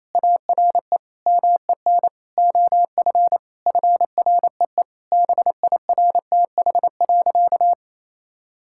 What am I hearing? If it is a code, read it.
Morse code, heard as a keyed tone, about 28 wpm, 700 Hz. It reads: ARE MEN OF FREE BIRTH.
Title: Sentences from Top 1000 Words - Review Part 5 - 28wpm